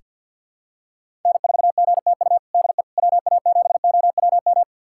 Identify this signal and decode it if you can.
Morse code, heard as a keyed tone, about 47 wpm, 700 Hz. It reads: N4GTU DE WA7OWM
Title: USA Callsign DE Callsign 50wpm